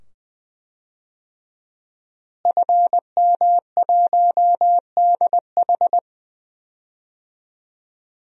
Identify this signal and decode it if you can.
Morse code, heard as a keyed tone, about 20 wpm, 700 Hz. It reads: FM1DH
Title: DXCC-100 Call Signs - 5 Characters - Repeated 2x - 20wpm